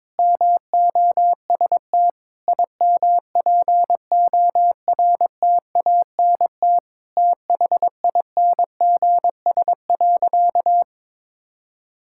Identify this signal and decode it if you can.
Morse code, heard as a keyed tone, about 22 wpm, 700 Hz. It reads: MOST IMPORTANT THINGS.
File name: practice / 22wpm / Sentences with 4-Word Phrases - 22wpm